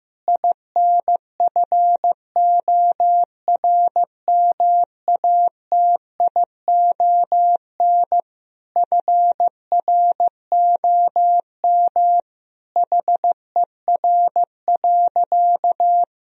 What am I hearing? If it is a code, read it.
Morse code, heard as a keyed tone, about 15 wpm, 700 Hz. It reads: INFORMATION FROM HER.
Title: Sentences from Top 1000 Words - Review Part 3 - 15wpm